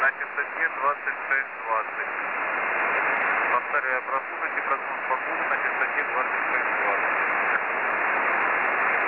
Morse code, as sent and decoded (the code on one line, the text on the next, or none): none